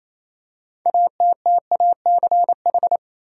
Morse code, heard as a keyed tone, about 28 wpm, 700 Hz. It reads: ATTACH
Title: Six Letter Words - 28wpm